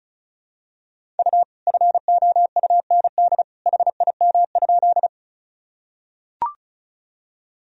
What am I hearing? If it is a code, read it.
Morse code, heard as a keyed tone, about 34 wpm, 700 Hz. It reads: U FOUND HIM?
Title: Sentences from Top 300 Words and Abbreviations 35wpm